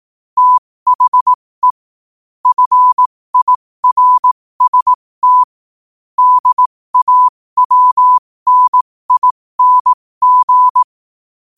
Morse code, heard as a keyed tone, about 18 wpm, spaced as for 15 wpm, 1000 Hz. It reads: THE FIRST DAWNING
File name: qotdfiles / 15wpm